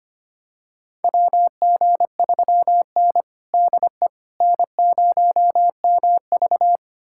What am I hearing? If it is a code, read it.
Morse code, heard as a keyed tone, about 25 wpm, 700 Hz. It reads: WG3N DE N0MV
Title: USA Callsign DE Callsign - 4 Characters - 25wpm